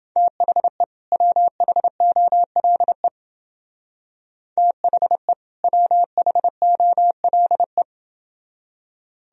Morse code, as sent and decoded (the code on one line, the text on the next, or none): - .... / .-- .... --- .-.. / - .... / .-- .... --- .-.. .
THE WHOLE THE WHOLE